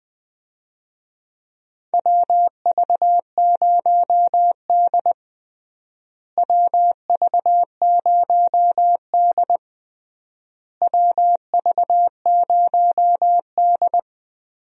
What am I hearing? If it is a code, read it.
Morse code, heard as a keyed tone, about 20 wpm, 700 Hz. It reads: WV0D WV0D WV0D
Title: USA Call Signs - 4 Characters - Repeated 3x - 20wpm